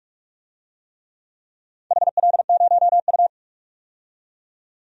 Morse code, heard as a keyed tone, about 48 wpm, 700 Hz. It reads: HL0U
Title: DXCC-100 Call Signs - 4 Characters - 45wpm